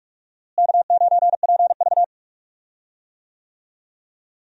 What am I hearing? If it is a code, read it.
Morse code, heard as a keyed tone, about 45 wpm, 700 Hz. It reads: K9PV
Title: USA Call Signs - 4 Characters - Repeated 2x - 45wpm